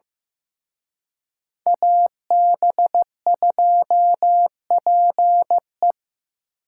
Morse code, heard as a keyed tone, about 18 wpm, 700 Hz. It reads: AB2PE
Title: Speed Racing - USA Call Signs 15wpm